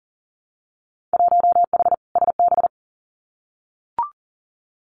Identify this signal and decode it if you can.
Morse code, heard as a keyed tone, about 42 wpm, 700 Hz. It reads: 1H SB
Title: Winter Field Day Exchanges 40wpm